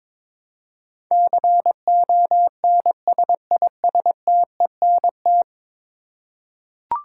Morse code, heard as a keyed tone, about 22 wpm, 700 Hz. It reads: CONSISTENT
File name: practice / 22wpm / 10 Letter Words - First 3 Letters Given - 22wpm